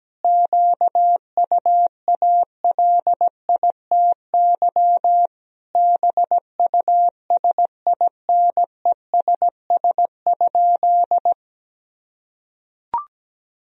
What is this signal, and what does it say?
Morse code, heard as a keyed tone, about 17 wpm, 700 Hz. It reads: QUALITY BUSINESS?
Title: Sentences from Top 1000 Words - Review Part 4 - 17wpm